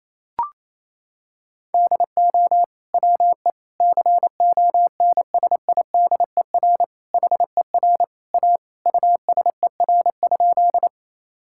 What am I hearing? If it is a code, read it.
Morse code, heard as a keyed tone, about 28 wpm, 700 Hz. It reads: DO WE CONSIDER HER A USER?
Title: Sentences from Top 1000 Words - Review Part 5 - 28wpm